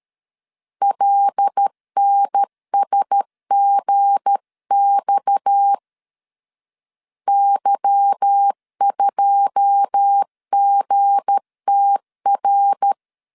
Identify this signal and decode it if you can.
Morse code, heard as a keyed tone, about 13 wpm, 800 Hz. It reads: LNSGX Y2GTR
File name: CW / DET / 1 9 1500C S C 12wpm 3811